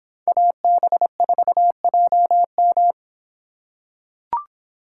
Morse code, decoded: AB4JM

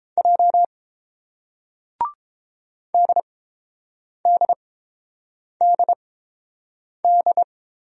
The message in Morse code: J DDDD

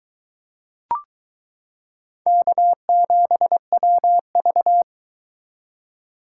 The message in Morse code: K7WV

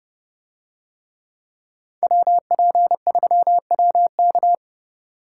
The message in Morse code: WP3WK

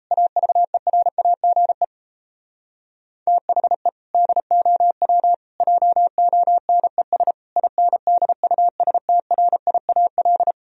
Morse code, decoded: AVERAGE THE DOW JONES INDUSTRIAL

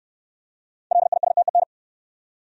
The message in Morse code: LISTEN

